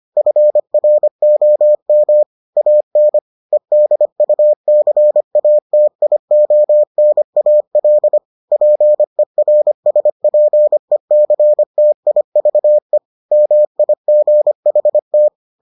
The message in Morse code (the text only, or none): FROM AN EDUCATIONAL PERSPECTIVE MIGHT